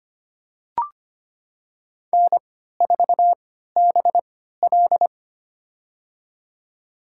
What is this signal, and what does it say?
Morse code, heard as a keyed tone, about 25 wpm, spaced as for 14 wpm, 700 Hz. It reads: N4BL
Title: USA Call Signs - 4 Characters - 3x Character Spacing - 25wpm